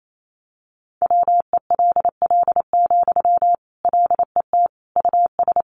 Morse code, heard as a keyed tone, about 28 wpm, 700 Hz. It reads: WELL, LET US
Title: Sentences from Top 1000 Words 28wpm